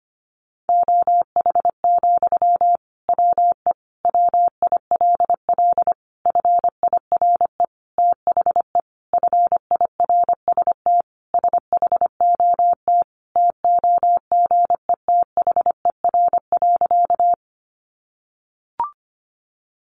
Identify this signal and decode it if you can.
Morse code, heard as a keyed tone, about 25 wpm, 700 Hz. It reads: OH, WE WILL FIRE THE FIRST SHOT TOGETHER.
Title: Sentences from Top 1100 Words 25wpm